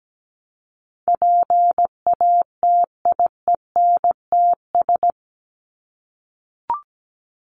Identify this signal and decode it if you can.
Morse code, heard as a keyed tone, about 17 wpm, 700 Hz. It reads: PATIENTS
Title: ES:S Suffix - Top 100 Words with Base Words 17wpm